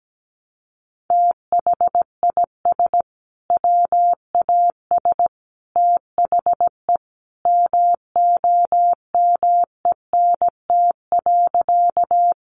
Morse code, heard as a keyed tone, about 17 wpm, 700 Hz. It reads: THIS WAS THE MOMENT.
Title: Sentences from Top 1000 Words 17wpm